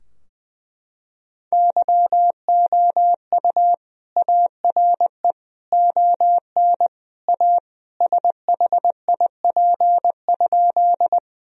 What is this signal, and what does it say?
Morse code, heard as a keyed tone, about 20 wpm, 700 Hz. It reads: YOU ARE ON A SHIP?
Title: Sentences from Top 1000 Words - Review Part 2 - 20wpm